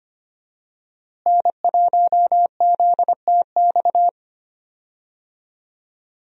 Morse code, heard as a keyed tone, about 25 wpm, 700 Hz. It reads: N1ZTX
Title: USA Call Signs - 5 Characters - Repeated 3x - 25wpm